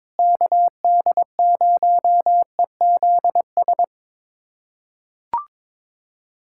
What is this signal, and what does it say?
Morse code, heard as a keyed tone, about 22 wpm, 700 Hz. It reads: KD0EZS
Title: USA Callsign DE Callsign 22wpm